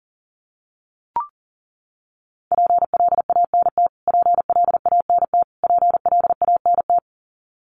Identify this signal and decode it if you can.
Morse code, heard as a keyed tone, about 40 wpm, 700 Hz. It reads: PLANT PLANT PLANT
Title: Top 500 Words Repeated 40wpm